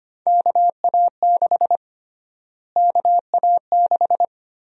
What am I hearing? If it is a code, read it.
Morse code, heard as a keyed tone, about 25 wpm, 700 Hz. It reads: KA6 KA6
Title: USA Call Sign Prefixes - Repeated 2x - 25wpm